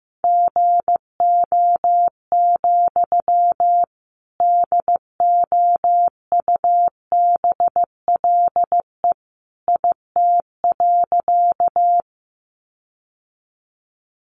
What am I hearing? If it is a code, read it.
Morse code, heard as a keyed tone, about 15 wpm, 700 Hz. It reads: GO, DOUBLE IT.
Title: Sentences from Top 1400 Words 15wpm